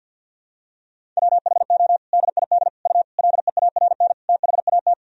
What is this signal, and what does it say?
Morse code, heard as a keyed tone, about 46 wpm, 700 Hz. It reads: WHO DID U LEARN THAT
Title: Sentences from Top 300 Words and Abbreviations 50wpm